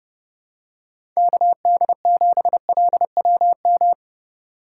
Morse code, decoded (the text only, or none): KD7LWM